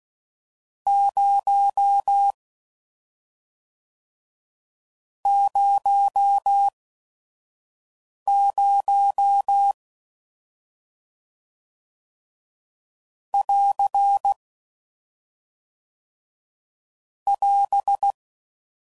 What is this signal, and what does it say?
Morse code, heard as a keyed tone, about 16 wpm, 800 Hz. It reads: 0 0 0 <AR> <AS>